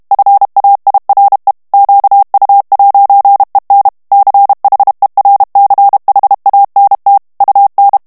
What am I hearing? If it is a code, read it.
Morse code, heard as a keyed tone, about 32 wpm, 800 Hz. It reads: FAIRE QU'EN CHERCHANT UN
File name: FRANCAIS TEXTE CLAIR 2 VITESSE 30 MOTS MINUTES